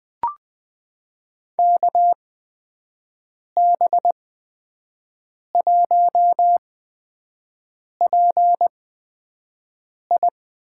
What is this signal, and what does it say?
Morse code, heard as a keyed tone, about 21 wpm, spaced as for 5 wpm, 700 Hz. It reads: KB1PI
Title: USA Call Signs - 8x Character Spacing - 20wpm